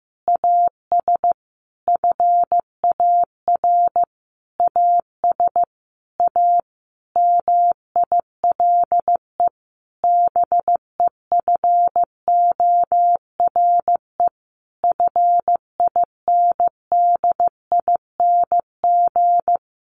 Morse code, heard as a keyed tone, about 15 wpm, 700 Hz. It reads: AS FAR AS A MILE BEFORE FINDING
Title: Sentences from Top 1100 Words 15wpm